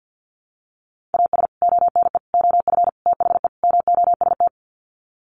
Morse code, heard as a keyed tone, about 50 wpm, 700 Hz. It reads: AS ONE OF THE MOST
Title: Sets of 5 Words 50wpm